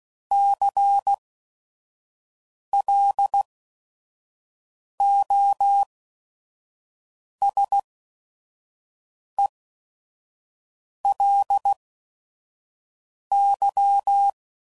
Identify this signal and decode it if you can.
Morse code, heard as a keyed tone, about 16 wpm, spaced as for 5 wpm, 800 Hz. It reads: CLOSELY
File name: CW Code Groups @5WPM